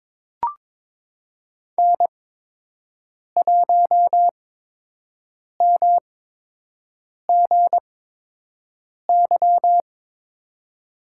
Morse code, heard as a keyed tone, about 22 wpm, spaced as for 6 wpm, 700 Hz. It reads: N1MGY
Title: USA Call Signs - 8x Character Spacing - 22wpm